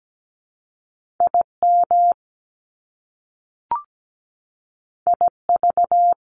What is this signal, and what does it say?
Morse code, heard as a keyed tone, about 17 wpm, 700 Hz. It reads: IM IV
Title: Top Two Letter Combinations in Words 17wpm